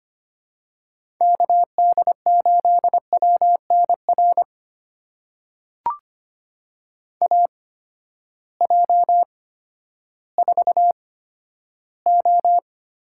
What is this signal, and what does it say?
Morse code, heard as a keyed tone, about 25 wpm, spaced as for 13 wpm, 700 Hz. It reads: KD8WNR AJ4O
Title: USA Call Signs - 8x Character Spacing - 25wpm